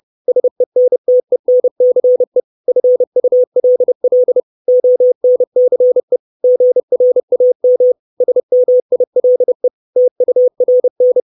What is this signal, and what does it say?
Morse code, heard as a keyed tone, about 30 wpm, 500 Hz. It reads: SENTENCE FULL ONCE GRAM SMILE TURN
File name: 632 Words 30WPM